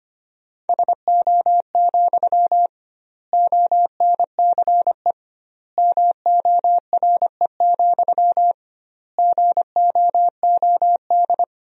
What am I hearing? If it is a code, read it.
Morse code, heard as a keyed tone, about 25 wpm, 700 Hz. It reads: SO, ONCE MORE, GOOD
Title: 2x Word-Spacing - Sentences from Top 300 Words - 25wpm